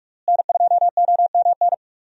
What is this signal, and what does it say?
Morse code, heard as a keyed tone, about 42 wpm, 700 Hz. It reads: N2OMN
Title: USA Call Signs - 5 Characters - Repeated 3x - 45wpm